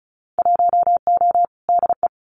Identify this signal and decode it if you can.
Morse code, heard as a keyed tone, about 34 wpm, 700 Hz. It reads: 1O DE
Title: Winter Field Day Exchanges 35wpm